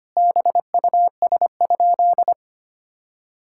Morse code, heard as a keyed tone, about 25 wpm, 700 Hz. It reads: BUS?
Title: Sentences from Top 1000 Words - Review Part 6 - 25wpm